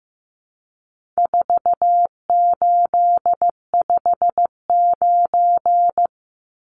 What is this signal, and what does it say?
Morse code, heard as a keyed tone, about 15 wpm, 700 Hz. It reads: 4859
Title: Numbers - 4-Digit 15wpm